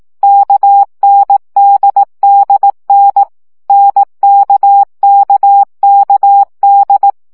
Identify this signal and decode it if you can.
Morse code, heard as a keyed tone, about 18 wpm, 800 Hz. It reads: KNDDN NKKKD
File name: CW gyakorlas n d k 15 baud